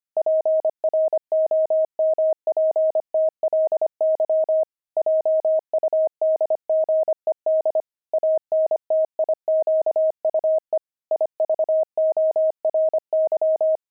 Morse code, heard as a keyed tone, about 25 wpm, 630 Hz. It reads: PROMPTLY JUDGED ANTIQUE IVORY
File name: pangram-fs-25-cs-25-buckles